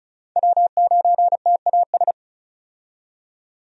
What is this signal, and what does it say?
Morse code, heard as a keyed tone, about 36 wpm, 700 Hz. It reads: W9TAS
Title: USA Call Signs - 5 Characters - 35wpm